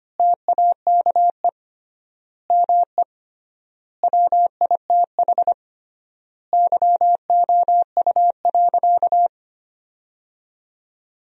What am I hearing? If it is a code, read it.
Morse code, heard as a keyed tone, about 25 wpm, 700 Hz. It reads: TAKE ME WITH YOU.
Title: 3x Word-Spacing - Sentences from Top 200 Words - 25wpm